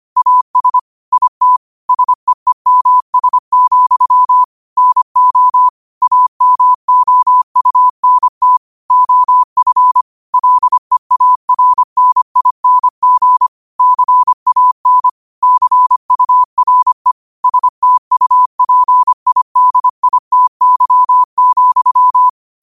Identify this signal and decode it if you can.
Morse code, heard as a keyed tone, about 25 wpm, 1000 Hz. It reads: AS IT SEEMS, NO AMOUNT OF LEARNING CAN CURE STUPIDITY,
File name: qotdfiles / 25wpm